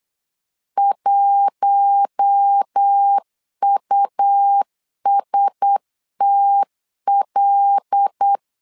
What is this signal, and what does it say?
Morse code, heard as a keyed tone, about 8 wpm, 800 Hz. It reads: 1USTL
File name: LEZIONI / 5 LBVC / 5 LB 8wpm 2816